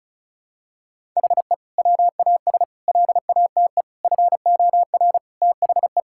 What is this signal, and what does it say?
Morse code, heard as a keyed tone, about 34 wpm, 700 Hz. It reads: HE WAS LATE FOR THE
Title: Sentences from Top 1600 Words 35wpm